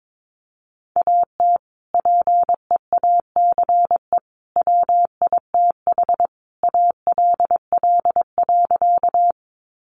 Morse code, heard as a keyed tone, about 22 wpm, 700 Hz. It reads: AT PEACE WITH ALL.
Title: Sentences from Top 1500 Words 22wpm